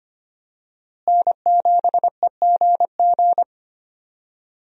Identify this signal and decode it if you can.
Morse code, heard as a keyed tone, about 25 wpm, 700 Hz. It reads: N7EGG
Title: USA Call Signs - 5 Characters - Repeated 3x - 25wpm